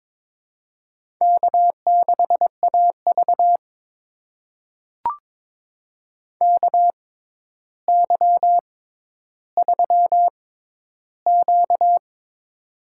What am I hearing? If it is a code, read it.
Morse code, heard as a keyed tone, about 22 wpm, spaced as for 11 wpm, 700 Hz. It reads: K6AV KY3Q
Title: USA Call Signs - 4 Characters - 6x Character Spacing - 22wpm